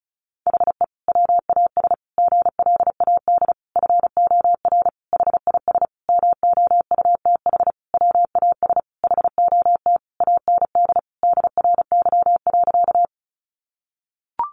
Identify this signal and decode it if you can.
Morse code, heard as a keyed tone, about 34 wpm, 700 Hz. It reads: HE WAS GLAD FOR HIS MOUTH WAS HOT AND DRY.
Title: Sentences from Top 1500 Words 35wpm